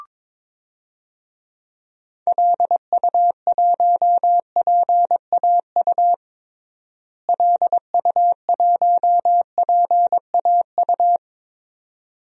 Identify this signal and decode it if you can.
Morse code, heard as a keyed tone, about 22 wpm, 700 Hz. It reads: LU1PAU LU1PAU
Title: DXCC-100 Call Signs - 6 Characters - Repeated 2x - 22wpm